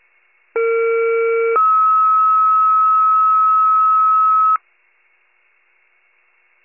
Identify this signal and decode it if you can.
Two-tone page, heard as a one-second tone, then a three-second tone, about 460 then 1290 Hz. It is heard from 0.6 to 4.6 s.
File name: pager / UNKNOWN / wcalert